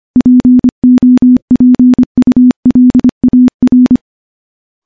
Morse code, heard as a keyed tone, about 25 wpm, 250 Hz. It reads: POPULAR
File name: morse-training-25WPM